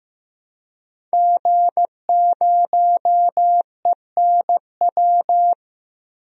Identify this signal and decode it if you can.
Morse code, heard as a keyed tone, about 15 wpm, 700 Hz. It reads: G0ENW
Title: DXCC-100 Call Signs - 5 Characters - 15wpm